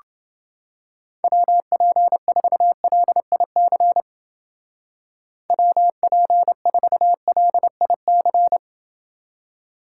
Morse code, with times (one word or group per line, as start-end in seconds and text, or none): 1.2-4.0 WP4LIC
5.5-8.6 WP4LIC